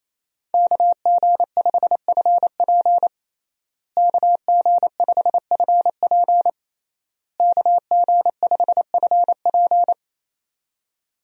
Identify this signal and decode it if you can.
Morse code, heard as a keyed tone, about 28 wpm, 700 Hz. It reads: KG5FP KG5FP KG5FP
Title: USA Call Signs - 5 Characters - Repeated 3x - 28wpm